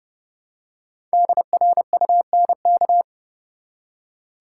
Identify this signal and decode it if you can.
Morse code, heard as a keyed tone, about 30 wpm, 700 Hz. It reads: DRUNK
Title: Five Letter Words - 30wpm